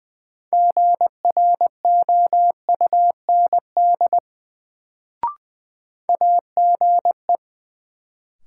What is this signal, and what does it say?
Morse code, heard as a keyed tone, about 21 wpm, 700 Hz. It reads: GROUND AGE
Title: Sentences from Top 500 Words - New Word Drill - 20wpm